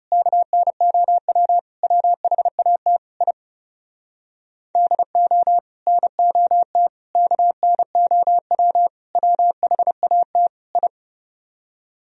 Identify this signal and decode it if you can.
Morse code, heard as a keyed tone, about 34 wpm, 700 Hz. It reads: KNOW WHAT I DO NOT KNOW WHAT I
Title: Speed-Racing - Sets of 5 Words 30wpm